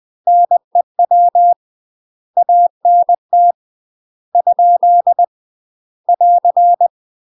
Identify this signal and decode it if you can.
Morse code, heard as a keyed tone, about 20 wpm, 700 Hz. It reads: NEW ANT ? <AR>
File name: UK MASTER QSOs TRAINING / 20WPM / UK MASTER QSOs 0053